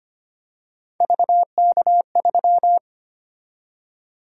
Morse code, heard as a keyed tone, about 25 wpm, 700 Hz. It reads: VK3
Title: USA Call Sign Prefixes - 25wpm